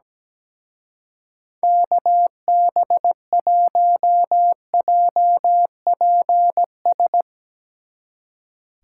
Morse code, heard as a keyed tone, about 17 wpm, 700 Hz. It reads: KB1JPS
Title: Speed Racing - USA 2x3 Call Signs 17wpm